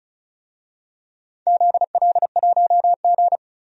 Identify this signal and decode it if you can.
Morse code, heard as a keyed tone, about 36 wpm, 700 Hz. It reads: ZL1G
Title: DXCC-100 Call Signs - 4 Characters - 35wpm